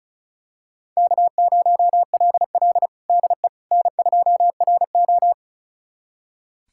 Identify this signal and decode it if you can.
Morse code, heard as a keyed tone, about 34 wpm, 700 Hz. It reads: K0LL DE N2RO
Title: USA Callsign DE Callsign - 4 Characters - 35wpm